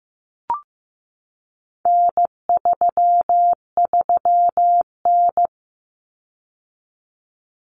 Morse code, heard as a keyed tone, about 15 wpm, 700 Hz. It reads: N33N
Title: Common Cut Numbers - 4-Digit 15wpm